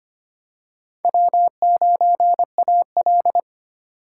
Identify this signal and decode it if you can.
Morse code, heard as a keyed tone, about 25 wpm, 700 Hz. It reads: W9AL
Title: USA Call Signs - 3x Character Spacing - 25wpm